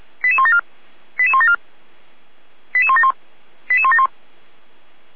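Five-tone selective call (ZVEI), heard as quick runs of five tones, tones of about 70 ms. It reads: 88174 88174 88171 88171
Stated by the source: Received by SDR; ZVEI five-tone call 88413.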